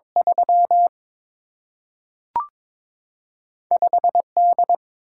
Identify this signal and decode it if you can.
Morse code, heard as a keyed tone, about 22 wpm, 700 Hz. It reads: 3 5 D
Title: Single-Letter Single-Number 22wpm